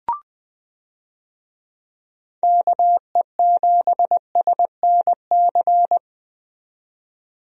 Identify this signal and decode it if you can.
Morse code, heard as a keyed tone, about 20 wpm, 700 Hz. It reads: KE7SNC
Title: USA Call Signs - 6 Characters - 20wpm